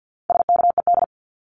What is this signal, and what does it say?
Morse code, heard as a keyed tone, about 50 wpm, 700 Hz. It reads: SKED
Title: Abbreviations Repeated 3x - 50wpm